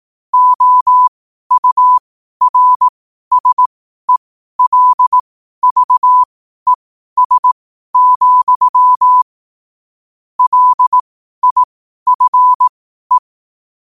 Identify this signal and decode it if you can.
Morse code, heard as a keyed tone, about 18 wpm, spaced as for 12 wpm, 1000 Hz. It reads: OURSELVES, LIFE
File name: qotdfiles / 12wpm